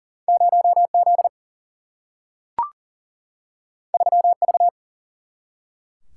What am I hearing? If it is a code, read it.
Morse code, heard as a keyed tone, about 40 wpm, 700 Hz. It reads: 0 Z 3 V
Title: Single-Letter Single-Number 40wpm